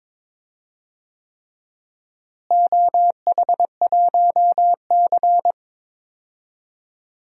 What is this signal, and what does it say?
Morse code, heard as a keyed tone, about 22 wpm, 700 Hz. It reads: OH1C
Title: DXCC-100 Call Signs - 4 Characters - Repeated 3x - 22wpm